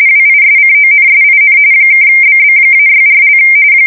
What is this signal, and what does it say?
RTTY, about 45.45 baud. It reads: WISCONSIN (BB-64)
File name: fsk170 / battleshipnj